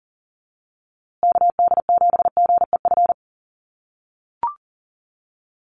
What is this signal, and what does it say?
Morse code, heard as a keyed tone, about 39 wpm, 700 Hz. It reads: KD7GEF
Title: USA Call Signs Repeated 3x 40wpm